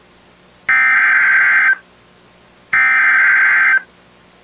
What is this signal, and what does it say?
Emergency Alert System (SAME) tones (header bursts), heard as harsh buzzing bursts. It is heard from 0.7 to 3.7 s.